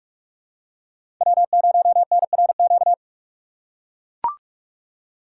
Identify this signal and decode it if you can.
Morse code, heard as a keyed tone, about 43 wpm, 700 Hz. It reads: W0NRQ
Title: USA Call Signs - 5 Characters - 2x Character Spacing - 45wpm